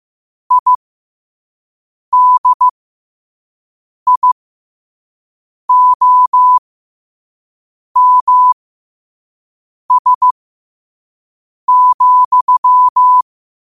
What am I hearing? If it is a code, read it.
Morse code, heard as a keyed tone, about 15 wpm, spaced as for 5 wpm, 1000 Hz. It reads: IDIOMS,
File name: qotdfiles / 5wpm